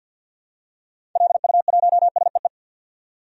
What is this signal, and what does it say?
Morse code, heard as a keyed tone, about 51 wpm, 700 Hz. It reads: LU1SEE
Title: DXCC-100 Call Signs - 6 Characters - 50wpm